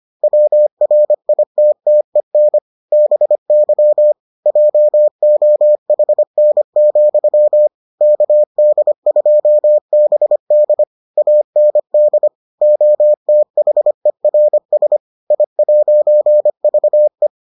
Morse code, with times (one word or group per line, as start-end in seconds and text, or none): 0.2-2.6 WRITTEN
2.9-4.1 BY
4.5-7.7 JOHN,
8.0-10.8 KD2BD
11.2-12.3 AND
12.6-15.0 OTHERS
15.3-17.3 I'VE